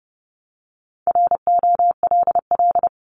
Morse code, heard as a keyed tone, about 30 wpm, 690 Hz. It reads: ROLL
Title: Multi-letter Phonemes 30wpm